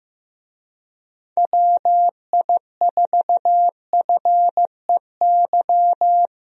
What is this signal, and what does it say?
Morse code, heard as a keyed tone, about 15 wpm, 700 Hz. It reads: WI4FEY
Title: USA Call Signs - 6 Characters - 15wpm